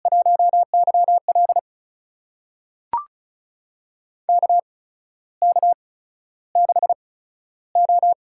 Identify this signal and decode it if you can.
Morse code, heard as a keyed tone, about 36 wpm, spaced as for 14 wpm, 700 Hz. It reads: 1YL KK6O
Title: USA Call Signs - 4 Characters - 8x Character Spacing - 35wpm